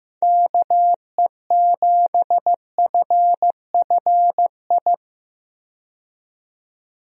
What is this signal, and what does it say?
Morse code, heard as a keyed tone, about 15 wpm, 700 Hz. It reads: KE7FFI
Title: USA Call Signs - 6 Characters - 15wpm